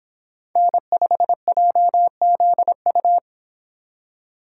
Morse code, heard as a keyed tone, about 25 wpm, 700 Hz. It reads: N5JZU